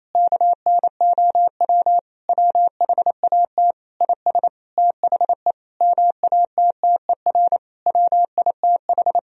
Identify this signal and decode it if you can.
Morse code, heard as a keyed tone, about 28 wpm, 700 Hz. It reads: KNOW WHAT IS THE MATTER WITH
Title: Sentences from Top 1000 Words - Review Part 4 - 28wpm